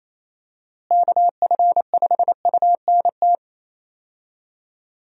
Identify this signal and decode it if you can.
Morse code, heard as a keyed tone, about 27 wpm, 700 Hz. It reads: KF5UNT
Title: USA Call Signs - 6 Characters - Repeated 3x - 28wpm